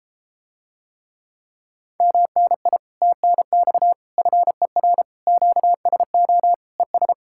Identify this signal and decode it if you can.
Morse code, heard as a keyed tone, about 33 wpm, 700 Hz. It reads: MNI TNX FER QSO ES